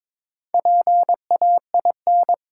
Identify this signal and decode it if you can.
Morse code, heard as a keyed tone, about 22 wpm, 700 Hz. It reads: PAIN
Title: Sentences from Top 1600 Words - New Word Drill - 22wpm